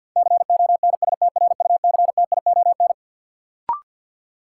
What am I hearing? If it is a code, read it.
Morse code, heard as a keyed tone, about 52 wpm, 700 Hz. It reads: CONSTRUCTION
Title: Sentences from Top 1600 Words - New Word Drill Alt - 50wpm